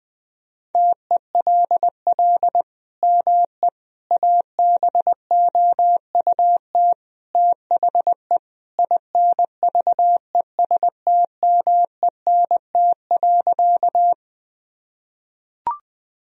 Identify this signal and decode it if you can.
Morse code, heard as a keyed tone, about 20 wpm, 700 Hz. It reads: TELL ME ABOUT THE INVESTMENT.
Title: Sentences from Top 1000 Words - Review Part 4 - 20wpm